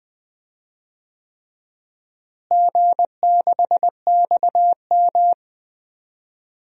Morse code, heard as a keyed tone, about 20 wpm, 700 Hz. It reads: G6XM
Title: DXCC-100 Call Signs - 4 Characters - Repeated 3x - 20wpm